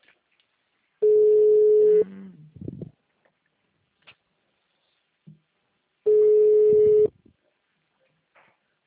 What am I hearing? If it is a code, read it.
Ringback tone (Europe, 425 Hz).
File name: out- -2010- - - .128